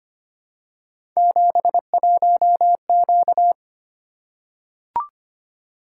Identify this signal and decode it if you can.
Morse code, heard as a keyed tone, about 25 wpm, 700 Hz. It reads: 71Q
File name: Three Letter-Number Combinations - 25wpm